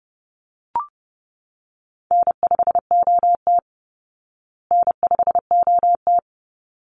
Morse code, heard as a keyed tone, about 30 wpm, 700 Hz. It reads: N5OT N5OT
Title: Call Signs Repeated 3x - 30wpm